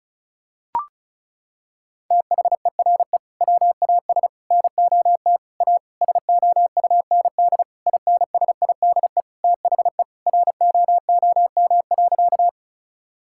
Morse code, heard as a keyed tone, about 34 wpm, 700 Hz. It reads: THERE WAS NOT A SOUND INSIDE THE ROOM.
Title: Sentences from Top 1000 Words - Review Part 6 - 35wpm